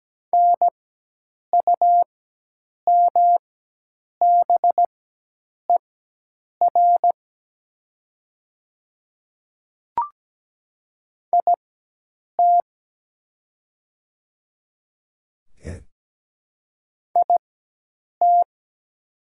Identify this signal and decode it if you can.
Morse code, heard as a keyed tone, about 17 wpm, spaced as for 8 wpm, 700 Hz. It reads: NUMBER IT IT
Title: 4x Character Spacing - Top 100 Words - 17wpm